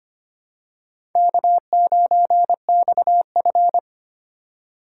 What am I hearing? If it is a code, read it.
Morse code, heard as a keyed tone, about 25 wpm, 700 Hz. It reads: K9XF
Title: USA Call Signs - 4 Characters - 3x Character Spacing - 25wpm